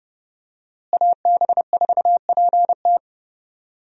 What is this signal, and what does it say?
Morse code, heard as a keyed tone, about 30 wpm, 700 Hz. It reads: AB4PT
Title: USA Call Signs - 5 Characters - Repeated 3x - 30wpm